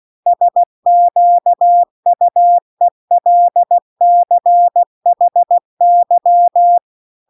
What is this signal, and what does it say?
Morse code, heard as a keyed tone, about 16 wpm, 700 Hz. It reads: SQUELCHY
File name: CW-The-Big-Twoheaded-River-2-16WPM